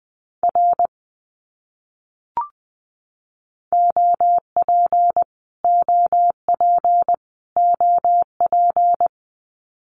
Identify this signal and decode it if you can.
Morse code, heard as a keyed tone, about 19 wpm, 700 Hz. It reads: R OP OP OP
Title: Abbreviations Repeated 3x - 20wpm